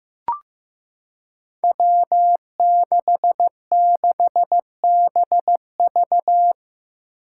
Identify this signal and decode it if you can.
Morse code, heard as a keyed tone, about 15 wpm, 700 Hz. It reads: W66BV
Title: Five Challenging Letter-Number Combinations 15wpm